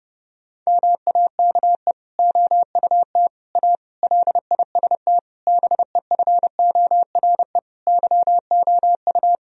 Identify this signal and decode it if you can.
Morse code, heard as a keyed tone, about 30 wpm, 700 Hz. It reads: MAKE OUT A LIST BEFORE YOU
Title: Sentences from Top 1000 Words - Review Part 2 - 30wpm